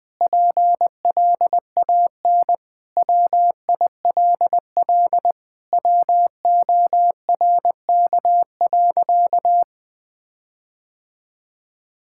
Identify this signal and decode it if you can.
Morse code, heard as a keyed tone, about 21 wpm, 700 Hz. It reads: PLAN WILL WORK.
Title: Sentences with 4-Word Phrases - 20wpm